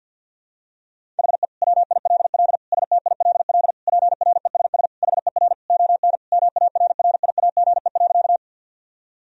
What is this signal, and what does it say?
Morse code, heard as a keyed tone, about 51 wpm, 700 Hz. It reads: HE WILL STILL PRESS HER ON MARRIAGE.
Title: Sentences from Top 1000 Words - Review Part 1 - 50wpm